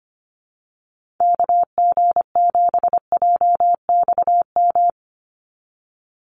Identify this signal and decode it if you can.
Morse code, heard as a keyed tone, about 25 wpm, 700 Hz. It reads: KG7JXM
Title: USA Call Signs 25wpm